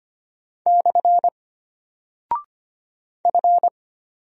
Morse code, heard as a keyed tone, about 25 wpm, 700 Hz. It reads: / F
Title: Single Character (Letter:Number:Punctuation:Prosign) Alt 25wpm